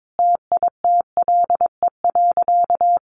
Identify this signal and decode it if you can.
Morse code, heard as a keyed tone, about 22 wpm, 700 Hz. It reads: TITLE.
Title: Sentences from Top 1300 Words 22wpm